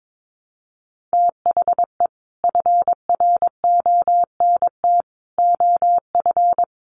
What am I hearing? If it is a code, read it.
Morse code, heard as a keyed tone, about 22 wpm, 700 Hz. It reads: THE FRONT OF
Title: Sets of 4 Words 22wpm